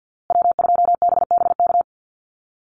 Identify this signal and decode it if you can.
Morse code, heard as a keyed tone, about 50 wpm, 700 Hz. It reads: W2BBK
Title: USA Call Signs Repeated 2x 50wpm